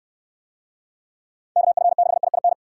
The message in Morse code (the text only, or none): KF6IIN